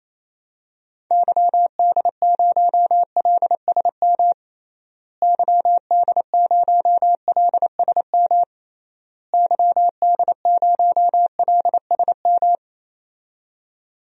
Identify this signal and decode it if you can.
Morse code, heard as a keyed tone, about 28 wpm, 700 Hz. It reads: YD0LSM YD0LSM YD0LSM
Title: DXCC-100 Call Signs - 6 Characters - Repeated 3x - 28wpm